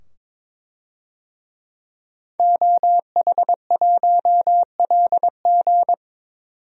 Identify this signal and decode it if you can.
Morse code, heard as a keyed tone, about 22 wpm, 700 Hz. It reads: OH1LG
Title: DXCC-100 Call Signs - 5 Characters - 22wpm